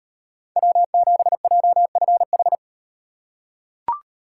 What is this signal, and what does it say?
Morse code, heard as a keyed tone, about 38 wpm, 700 Hz. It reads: W7JFH